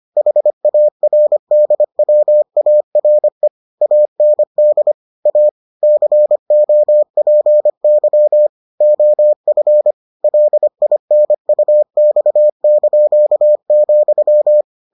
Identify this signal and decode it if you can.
Morse code, heard as a keyed tone, about 25 wpm, 600 Hz. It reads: HARDWARE AND A COPY OF LINUX),